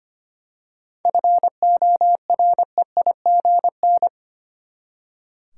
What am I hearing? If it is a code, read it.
Morse code, heard as a keyed tone, about 25 wpm, 700 Hz. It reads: FOREIGN
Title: Seven Letter Words - 25wpm